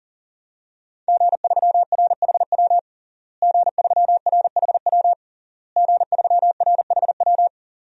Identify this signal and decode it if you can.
Morse code, heard as a keyed tone, about 40 wpm, 700 Hz. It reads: G3RHW G3RHW G3RHW
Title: DXCC-100 Call Signs - 5 Characters - Repeated 3x - 40wpm